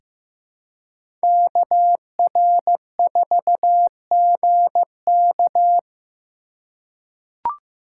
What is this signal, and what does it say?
Morse code, heard as a keyed tone, about 16 wpm, 700 Hz. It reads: KR4GK
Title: USA Call Signs - 2x Character Spacing - 15wpm